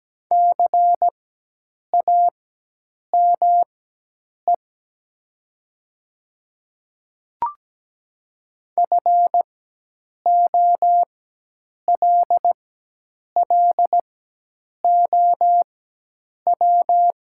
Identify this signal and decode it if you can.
Morse code, heard as a keyed tone, about 17 wpm, spaced as for 8 wpm, 700 Hz. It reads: CAME FOLLOW
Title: 4x Character Spacing - Top 200 Words - 17wpm